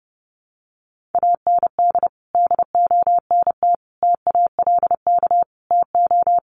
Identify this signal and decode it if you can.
Morse code, heard as a keyed tone, about 30 wpm, 700 Hz. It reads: AND DONT TALK TO
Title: Sentences from Top 400 Words 30wpm